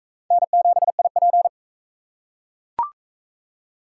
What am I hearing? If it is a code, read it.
Morse code, heard as a keyed tone, about 41 wpm, 700 Hz. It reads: N7IP